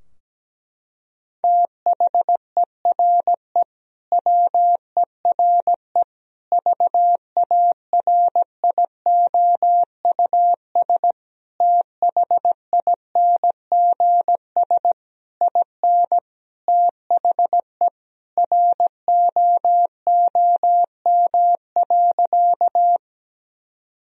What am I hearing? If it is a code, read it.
Morse code, heard as a keyed tone, about 17 wpm, 700 Hz. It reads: THERE WERE VARIOUS THINGS IN THE ROOM.
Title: Sentences from Top 1000 Words - Review Part 4 - 17wpm